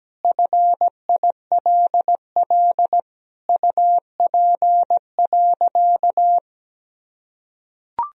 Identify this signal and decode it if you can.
Morse code, heard as a keyed tone, about 17 wpm, 700 Hz. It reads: FILL UP.
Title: Sentences from Top 1000 Words - Review Part 3 - 17wpm